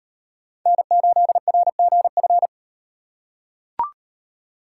Morse code, heard as a keyed tone, about 38 wpm, 700 Hz. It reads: N8RGF